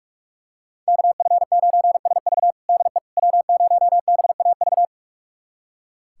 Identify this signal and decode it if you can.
Morse code, heard as a keyed tone, about 41 wpm, 700 Hz. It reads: KF9SV DE W0BAV
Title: USA Callsign DE Callsign - 5 Characters - 45wpm